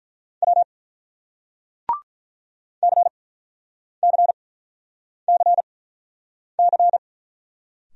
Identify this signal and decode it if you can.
Morse code, heard as a keyed tone, about 46 wpm, spaced as for 9 wpm, 700 Hz. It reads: W CCCC